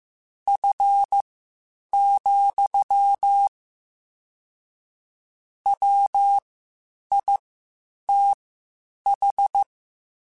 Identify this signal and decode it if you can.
Morse code, heard as a keyed tone, about 15 wpm, spaced as for 8 wpm, 800 Hz. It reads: F, WITH
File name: Spaced Characters / Spaced Characters-mp3s / Weather 4 14 sp @13